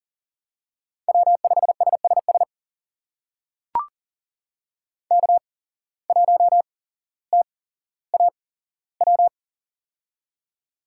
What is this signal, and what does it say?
Morse code, heard as a keyed tone, about 42 wpm, spaced as for 17 wpm, 700 Hz. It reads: W5SSS K1TAW
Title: USA Call Signs - 5 Characters - 8x Character Spacing - 40wpm